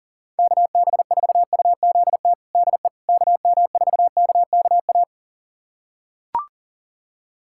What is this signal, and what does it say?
Morse code, heard as a keyed tone, about 37 wpm, 700 Hz. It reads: KB4UZT DE KM4KKA
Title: USA Callsign DE Callsign - 6 Characters - 40wpm